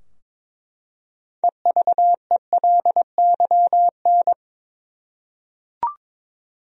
Morse code, heard as a keyed tone, about 22 wpm, 700 Hz. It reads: EVELYN
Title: USA Common Names - 22wpm